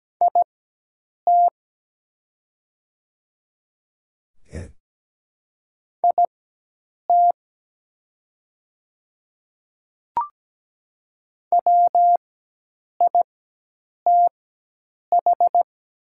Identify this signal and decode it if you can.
Morse code, heard as a keyed tone, about 17 wpm, spaced as for 8 wpm, 700 Hz. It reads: IT IT WITH
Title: 4x Character Spacing - Top 200 Words - 17wpm